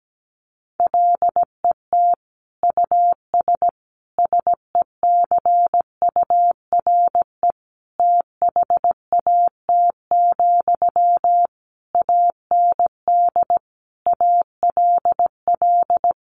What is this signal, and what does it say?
Morse code, heard as a keyed tone, about 17 wpm, 700 Hz. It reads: LET US SECURE THAT, AND ALL
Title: Sentences from Top 1200 Words 17wpm